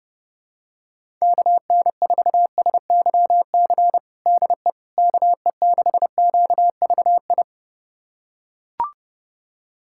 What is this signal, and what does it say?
Morse code, heard as a keyed tone, about 30 wpm, 700 Hz. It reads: KN4SYC DE KE6QVI
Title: USA Callsign DE Callsign - 6 Characters - 30wpm